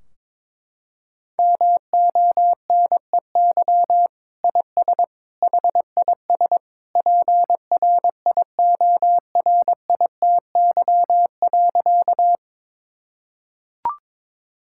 Morse code, heard as a keyed tone, about 22 wpm, 700 Hz. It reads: MONEY IS HIS PRIORITY.
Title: Sentences from Top 1600 Words 22wpm